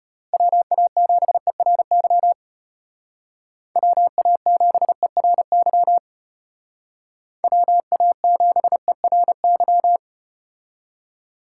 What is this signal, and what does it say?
Morse code, heard as a keyed tone, about 33 wpm, 700 Hz. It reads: WA7ERY WA7ERY WA7ERY